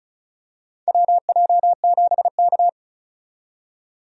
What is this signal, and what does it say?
Morse code, heard as a keyed tone, about 36 wpm, 700 Hz. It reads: WJ7K
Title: USA Call Signs - 4 Characters - 35wpm